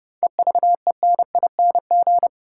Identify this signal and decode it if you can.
Morse code, heard as a keyed tone, about 30 wpm, 700 Hz. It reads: EVENING
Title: Seven Letter Words - 30wpm